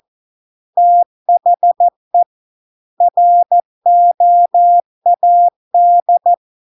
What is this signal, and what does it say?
Morse code, heard as a keyed tone, about 14 wpm, 700 Hz. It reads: THE ROAD